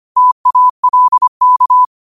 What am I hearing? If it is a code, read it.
Morse code, heard as a keyed tone, about 25 wpm, 1000 Hz. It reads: TALK
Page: Quotes for Wed, 13 Aug 2025 in Morse Code at 25 words per minute.